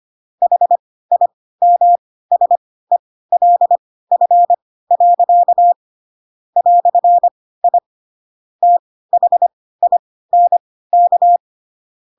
Morse code, heard as a keyed tone, about 25 wpm, spaced as for 16 wpm, 700 Hz. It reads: HIMSELF. "I THINK
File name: Pinocchio / Pinocchio ch3 10 25 ik1hge